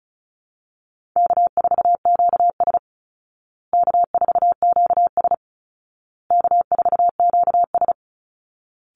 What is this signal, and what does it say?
Morse code, heard as a keyed tone, about 34 wpm, 700 Hz. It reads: K4QS K4QS K4QS
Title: Call Signs Repeated 2x - 35wpm